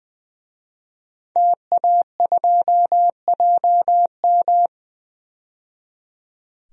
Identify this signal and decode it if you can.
Morse code, heard as a keyed tone, about 20 wpm, 700 Hz. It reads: TA2JM
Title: DXCC-100 Call Signs - 5 Characters - 20wpm